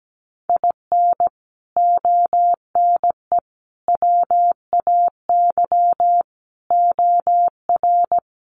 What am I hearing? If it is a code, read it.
Morse code, heard as a keyed tone, about 17 wpm, 700 Hz. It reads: IN ONE WAY OR
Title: Sets of 5 Words Encore 17wpm